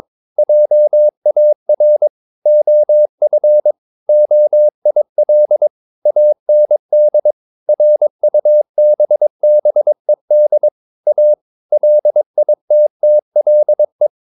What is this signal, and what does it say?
Morse code, heard as a keyed tone, about 22 wpm, 600 Hz. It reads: JAR OF OIL AND RUBBED A LITTLE
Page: Romanen Never af Ken Follet på engelsk.